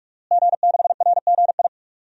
Morse code, heard as a keyed tone, about 42 wpm, 700 Hz. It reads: G6RGI
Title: DXCC-100 Call Signs - 5 Characters - Repeated 2x - 45wpm